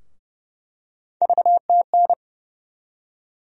Morse code, heard as a keyed tone, about 30 wpm, 700 Hz. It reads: VTN
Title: USA Call Sign Postfixes - 3 Letters 30wpm